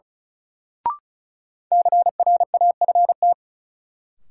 Morse code, heard as a keyed tone, about 36 wpm, 700 Hz. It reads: CRAFT
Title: Sentences from Top 1000 Words - Review Part 6 - New Word Drill - 35wpm